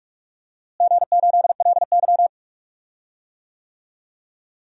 Morse code, heard as a keyed tone, about 46 wpm, 700 Hz. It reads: G8LY
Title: DXCC-100 Call Signs - 4 Characters - Repeated 3x - 45wpm